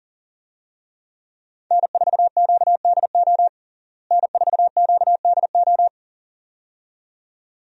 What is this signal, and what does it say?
Morse code, heard as a keyed tone, about 37 wpm, 700 Hz. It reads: N4QDO N4QDO
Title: USA Call Signs - 5 Characters - Repeated 2x - 40wpm